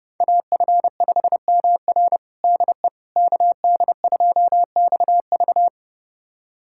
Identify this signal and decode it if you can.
Morse code, heard as a keyed tone, about 32 wpm, 700 Hz. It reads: AF5MR DE KD2XV
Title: USA Callsign DE Callsign - 5 Characters - 30wpm